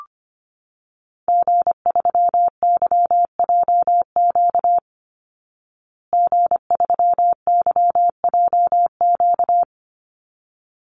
Morse code, heard as a keyed tone, about 25 wpm, 700 Hz. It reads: G3YJQ G3YJQ
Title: Call Signs Repeated 3x - 25wpm